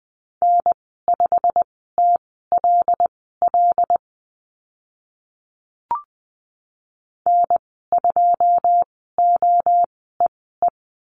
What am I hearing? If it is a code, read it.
Morse code, heard as a keyed tone, about 20 wpm, spaced as for 14 wpm, 700 Hz. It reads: N5TLL N2OEE
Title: 088 Lesson 15g - Number 5 - Call Signs 20wpm